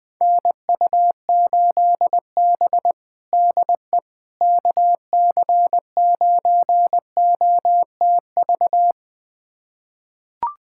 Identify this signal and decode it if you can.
Morse code, heard as a keyed tone, about 21 wpm, 700 Hz. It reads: NU8B DE KC9OTV
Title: USA Callsign DE Callsign 20wpm